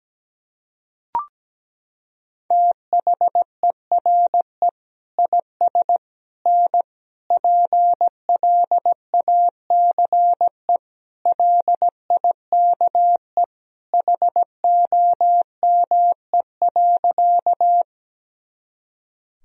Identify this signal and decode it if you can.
Morse code, heard as a keyed tone, about 17 wpm, 700 Hz. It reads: THERE IS N PLACE LIKE HOME.
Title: Sentences from Top 200 Words and Abbreviations 17wpm